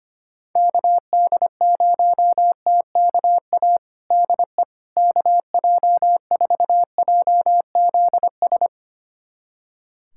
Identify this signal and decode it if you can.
Morse code, heard as a keyed tone, about 25 wpm, 700 Hz. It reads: KD0TKA DE KJ4JZS
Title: USA Callsign DE Callsign - 6 Characters - 25wpm